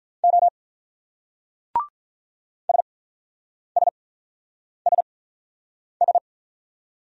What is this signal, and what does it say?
Morse code, heard as a keyed tone, about 41 wpm, spaced as for 8 wpm, 700 Hz. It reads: O SSSS
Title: Speed-Racing Single Letters 35wpm